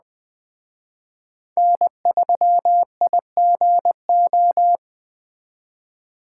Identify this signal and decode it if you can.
Morse code, heard as a keyed tone, about 20 wpm, 700 Hz. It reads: N3IGO